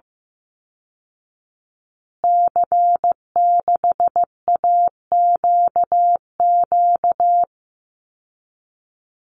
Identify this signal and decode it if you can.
Morse code, heard as a keyed tone, about 15 wpm, 700 Hz. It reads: C6AQQ
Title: Call Signs Repeated 2x - 15wpm